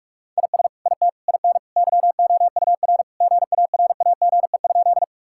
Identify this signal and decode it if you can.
Morse code, heard as a keyed tone, about 43 wpm, 700 Hz. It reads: IS IT IN YOUR GARAGE?
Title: Sentences from Top 1600 Words 45wpm